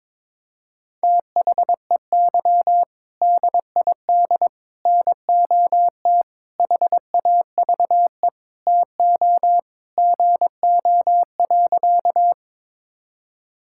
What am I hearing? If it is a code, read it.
Morse code, heard as a keyed tone, about 22 wpm, 700 Hz. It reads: THEY DID NOT HAVE TO GO.
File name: practice / 22wpm / Sentences with 4-Word Phrases - 22wpm